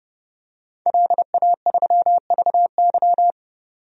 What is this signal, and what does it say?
Morse code, heard as a keyed tone, about 30 wpm, 700 Hz. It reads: LA3VY
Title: DXCC-100 Call Signs - 5 Characters - 30wpm